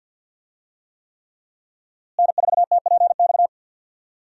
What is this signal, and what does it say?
Morse code, heard as a keyed tone, about 52 wpm, 700 Hz. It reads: N4TPX
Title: USA Call Signs - 5 Characters - Repeated 3x - 50wpm